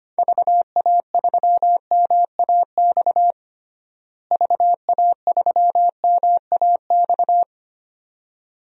Morse code, heard as a keyed tone, about 25 wpm, 700 Hz. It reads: VA3MAX VA3MAX
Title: DXCC-100 Call Signs - 6 Characters - Repeated 2x - 25wpm